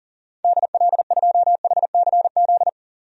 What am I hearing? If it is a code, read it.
Morse code, heard as a keyed tone, about 40 wpm, 700 Hz. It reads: DL2HCZ
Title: DXCC-100 Call Signs - 6 Characters - 40wpm